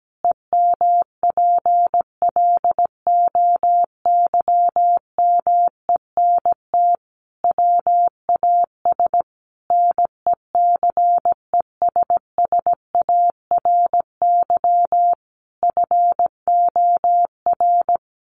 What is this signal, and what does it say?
Morse code, heard as a keyed tone, about 17 wpm, 700 Hz. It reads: EMPLOYMENT WAS NECESSARY FOR
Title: Sentences from Top 1500 Words 17wpm